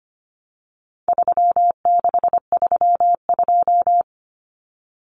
Morse code, heard as a keyed tone, about 25 wpm, 700 Hz. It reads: 3632
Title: Common Cut Numbers - 4-Digit 25wpm